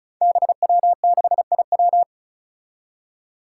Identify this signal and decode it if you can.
Morse code, heard as a keyed tone, about 36 wpm, 700 Hz. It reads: BW6IW
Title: Five Challenging Letter-Number Combinations 35wpm